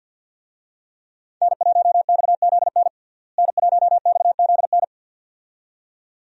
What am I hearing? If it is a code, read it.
Morse code, heard as a keyed tone, about 53 wpm, 700 Hz. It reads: N1XZN N1XZN
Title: USA Call Signs - 5 Characters - Repeated 2x - 50wpm